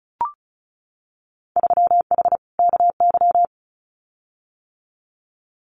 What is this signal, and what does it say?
Morse code, heard as a keyed tone, about 34 wpm, 700 Hz. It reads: 3H KY
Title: Winter Field Day Exchanges Encore 35wpm